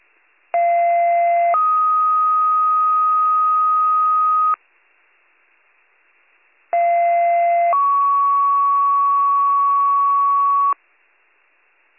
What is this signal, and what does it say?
Two-tone page, heard as a one-second tone, then a three-second tone, about 690 then 1250 Hz, about 690 then 1120 Hz, with noise. It is heard from 0.5 to 10.7 s.